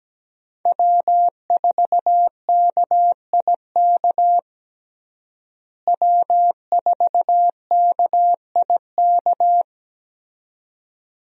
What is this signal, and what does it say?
Morse code, heard as a keyed tone, about 17 wpm, 700 Hz. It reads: W4KIK W4KIK
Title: USA Call Signs - 5 Characters - Repeated 2x - 17wpm